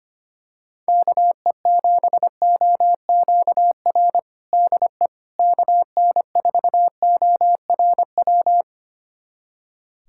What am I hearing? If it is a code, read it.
Morse code, heard as a keyed tone, about 25 wpm, 700 Hz. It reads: KE7OQR DE KN4ORW
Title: USA Callsign DE Callsign - 6 Characters - 25wpm